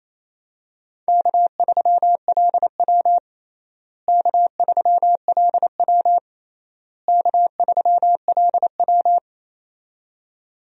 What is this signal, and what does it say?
Morse code, heard as a keyed tone, about 28 wpm, 700 Hz. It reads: K3LW K3LW K3LW
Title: USA Call Signs - 4 Characters - Repeated 3x - 28wpm